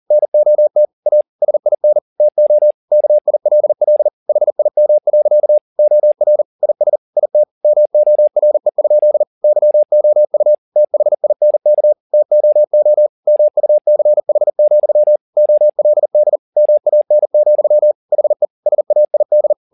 Morse code, heard as a keyed tone, about 40 wpm, 600 Hz. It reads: NOT A SIN TO KILL HIM. OR IS IT MORE? YOU THINK TOO MUCH, OLD MAN, HE SAID
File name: the old man and the sea 2 40wpm